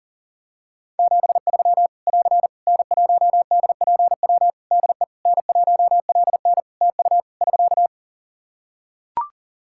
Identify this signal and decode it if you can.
Morse code, heard as a keyed tone, about 40 wpm, 700 Hz. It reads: 73 <AR> N1DPW DE N1LN TU <SK>
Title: Practice QSO - Exchange 4 - Ending - 40wpm